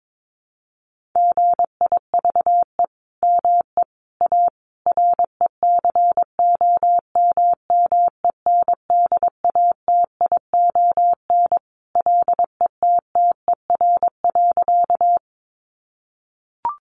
Morse code, heard as a keyed tone, about 22 wpm, 700 Hz. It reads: GIVE ME A RECOMMENDATION LETTER.
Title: Sentences from Top 1400 Words 22wpm